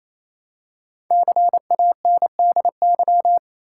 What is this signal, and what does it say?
Morse code, heard as a keyed tone, about 28 wpm, 700 Hz. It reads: CANDY
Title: Five Letter Words - 28wpm